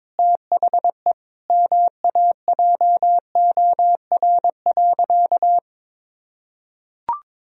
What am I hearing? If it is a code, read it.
Morse code, heard as a keyed tone, about 22 wpm, 700 Hz. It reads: THE MAJOR.
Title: Sentences from Top 1000 Words - Review Part 4 - 22wpm